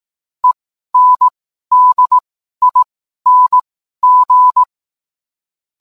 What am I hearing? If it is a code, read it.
Morse code, heard as a keyed tone, about 18 wpm, spaced as for 12 wpm, 1000 Hz. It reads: ENDING